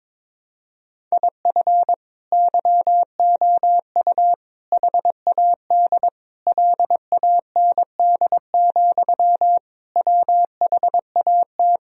Morse code, heard as a keyed tone, about 22 wpm, 700 Hz. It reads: IF YOU HAD LAND, WHAT
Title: Sentences from Top 1000 Words - Review Part 1 - 22wpm